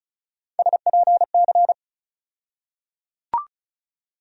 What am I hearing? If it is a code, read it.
Morse code, heard as a keyed tone, about 34 wpm, 700 Hz. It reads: SPC
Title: USA Call Sign Postfixes - 3 Letters 35wpm